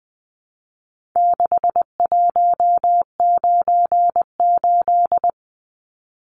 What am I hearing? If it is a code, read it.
Morse code, heard as a keyed tone, about 20 wpm, 700 Hz. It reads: 6198
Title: Numbers - 4-Digit 20wpm